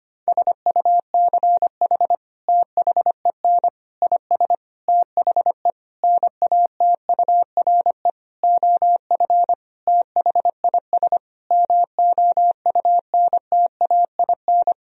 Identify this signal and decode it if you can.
Morse code, heard as a keyed tone, about 25 wpm, 700 Hz. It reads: SUCH THEN IS THE NATURE OF THIS MOUNTAIN
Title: Sentences from Top 1000 Words - Review Part 4 - 25wpm